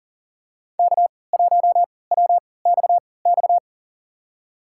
Morse code, heard as a keyed tone, about 40 wpm, spaced as for 23 wpm, 700 Hz. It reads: K1WXX
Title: USA Call Signs - 5 Characters - 3x Character Spacing - 40wpm